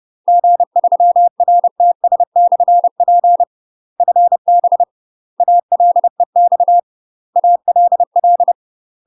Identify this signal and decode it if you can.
Morse code, heard as a keyed tone, about 30 wpm, 700 Hz. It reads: G3RTS/P FB ALEX ALL
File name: UK MASTER QSOs TRAINING / 30WPM / UK MASTER QSOs 0003